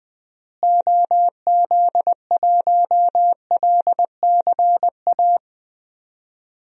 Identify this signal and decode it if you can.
Morse code, heard as a keyed tone, about 20 wpm, 700 Hz. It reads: OZ1LCA
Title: DXCC-100 Call Signs - 6 Characters - Repeated 2x - 20wpm